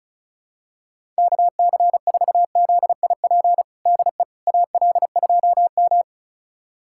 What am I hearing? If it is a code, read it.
Morse code, heard as a keyed tone, about 36 wpm, 700 Hz. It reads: KC4ZIP DE AL2M
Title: USA Callsign DE Callsign 35wpm